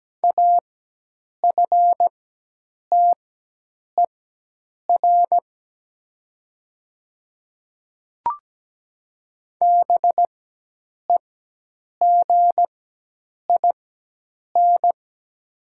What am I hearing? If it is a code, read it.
Morse code, heard as a keyed tone, about 17 wpm, spaced as for 8 wpm, 700 Hz. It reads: AFTER BEGIN
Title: 4x Character Spacing - Top 500 Words - 17wpm